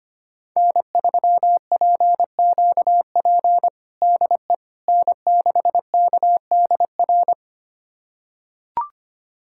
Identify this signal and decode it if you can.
Morse code, heard as a keyed tone, about 25 wpm, 700 Hz. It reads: N3PQP DE N6KDR
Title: USA Callsign DE Callsign - 5 Characters - 25wpm